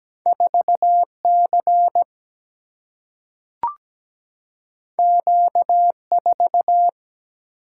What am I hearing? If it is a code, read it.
Morse code, heard as a keyed tone, about 17 wpm, 700 Hz. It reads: 4C Q4
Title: Single-Letter Single-Number 17wpm